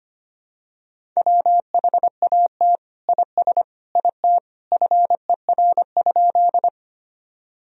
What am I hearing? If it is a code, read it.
Morse code, heard as a keyed tone, about 25 wpm, 700 Hz. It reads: WHAT IS IT FER?
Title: Sentences from Top 100 Words and Abbreviations 25wpm